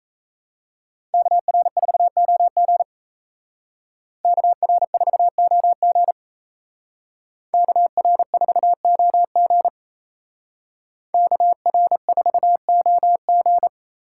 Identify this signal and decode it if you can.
Morse code, heard as a keyed tone, about 34 wpm, 700 Hz. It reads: KR4OG KR4OG KR4OG KR4OG
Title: Speed Racing - USA Call Signs 28wpm